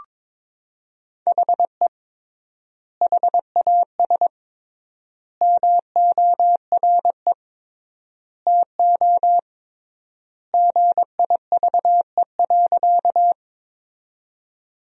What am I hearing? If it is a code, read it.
Morse code, heard as a keyed tone, about 22 wpm, 700 Hz. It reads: HE HAS MORE TO GIVE.
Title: 3x Word-Spacing - Sentences from Top 200 Words - 22wpm